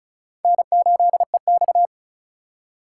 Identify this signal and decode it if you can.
Morse code, heard as a keyed tone, about 46 wpm, 700 Hz. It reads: N8EX